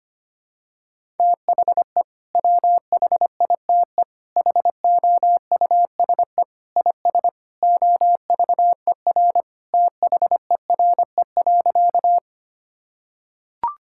Morse code, heard as a keyed tone, about 25 wpm, 700 Hz. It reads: THE WHITE HOUSE IS OVER THERE.
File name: practice / 25wpm / Sentences with 3-Word Phrases - 25wpm